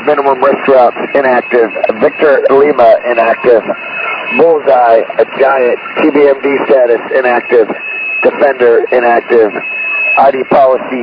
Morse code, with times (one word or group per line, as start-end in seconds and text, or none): none